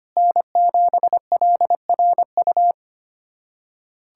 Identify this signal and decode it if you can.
Morse code, heard as a keyed tone, about 25 wpm, 700 Hz. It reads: N7LRU
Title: USA Callsign DE Callsign 25wpm